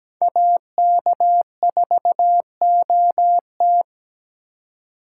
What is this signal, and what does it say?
Morse code, heard as a keyed tone, about 20 wpm, 700 Hz. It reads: AK4OT
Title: Speed Racing - USA 2x2 Call Signs 17wpm